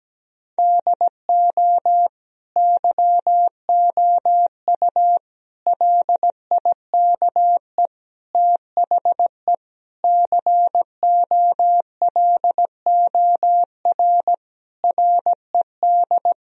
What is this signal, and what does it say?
Morse code, heard as a keyed tone, about 17 wpm, 700 Hz. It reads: DO YOU LIKE THE COLOR RED
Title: Sentences from Top 1000 Words - Review Part 2 - 17wpm